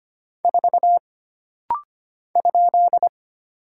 Morse code, heard as a keyed tone, about 25 wpm, 700 Hz. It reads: 4 ?
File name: Single Character (Letter:Number:Punctuation:Prosign) - Rapid-Fire - 25wpm